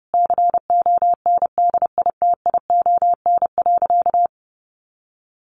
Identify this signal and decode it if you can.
Morse code, heard as a keyed tone, about 30 wpm, 700 Hz. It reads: CONDITION.
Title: Sentences from Top 1100 Words 30wpm